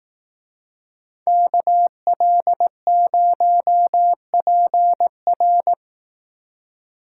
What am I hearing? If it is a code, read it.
Morse code, heard as a keyed tone, about 18 wpm, 700 Hz. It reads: KL0PR
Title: Speed Racing - USA 2x2 Call Signs 15wpm